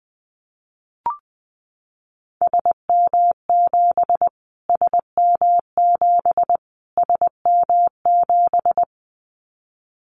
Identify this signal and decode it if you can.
Morse code, heard as a keyed tone, about 20 wpm, 700 Hz. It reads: SM7 SM7 SM7
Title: Callsign Prefixes for Top 25 DXCC - Repeated 3x - 20wpm